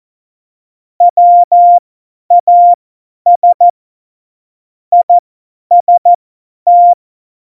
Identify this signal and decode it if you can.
Morse code, heard as a keyed tone, about 14 wpm, spaced as for 10 wpm, 700 Hz. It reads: WAS IST
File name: Morsekurs / Witze / Witz 18 Juli 17, Tempo 50 F70